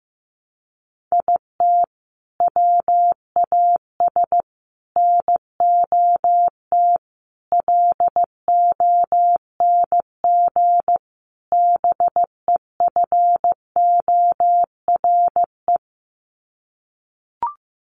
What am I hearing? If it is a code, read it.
Morse code, heard as a keyed tone, about 15 wpm, 700 Hz. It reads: IT WAS NOT LONG BEFORE
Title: Sets of 5 Words Encore2 15wpm